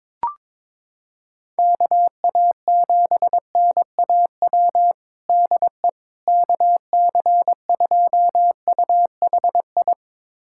Morse code, heard as a keyed tone, about 22 wpm, 700 Hz. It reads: KA7NAW DE KC2UHI
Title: USA Callsign DE Callsign 22wpm